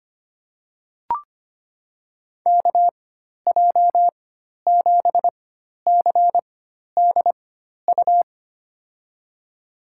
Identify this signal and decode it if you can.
Morse code, heard as a keyed tone, about 25 wpm, spaced as for 12 wpm, 700 Hz. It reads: KJ7CDU
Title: USA Call Signs - 4x Character Spacing - 25wpm